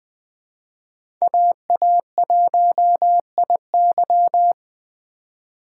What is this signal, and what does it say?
Morse code, heard as a keyed tone, about 21 wpm, 700 Hz. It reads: AA1IY
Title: Speed Racing - USA Call Signs 17wpm